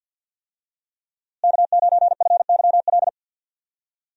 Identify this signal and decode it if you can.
Morse code, heard as a keyed tone, about 51 wpm, 700 Hz. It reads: K9FYL
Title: USA Call Signs - 5 Characters - Repeated 2x - 50wpm